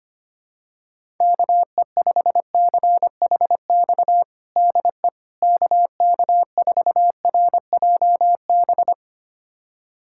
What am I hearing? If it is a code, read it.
Morse code, heard as a keyed tone, about 25 wpm, 700 Hz. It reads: KE5CHX DE KK4RJB
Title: USA Callsign DE Callsign - 6 Characters - 25wpm